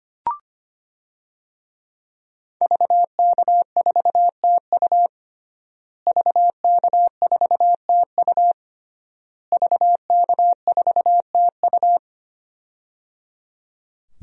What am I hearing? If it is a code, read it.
Morse code, heard as a keyed tone, about 25 wpm, 700 Hz. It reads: VK4TU VK4TU VK4TU
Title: DXCC-100 Call Signs - 5 Characters - Repeated 3x - 25wpm